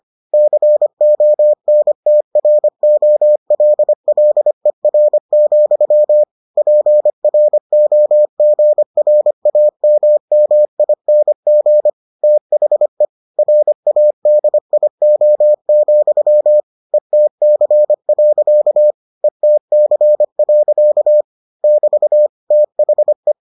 Morse code, heard as a keyed tone, about 25 wpm, 600 Hz. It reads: CONTROLLER, PROGRAMMING THE RADIO, ETC. ETC. <BT> THE